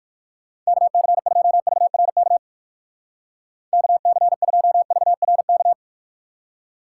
Morse code, heard as a keyed tone, about 56 wpm, 700 Hz. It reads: KC2VRK KC2VRK